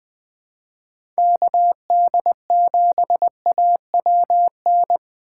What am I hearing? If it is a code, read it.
Morse code, heard as a keyed tone, about 20 wpm, 700 Hz. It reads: KD7AWN
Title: USA Call Signs - 6 Characters - 6x Character Spacing - 20wpm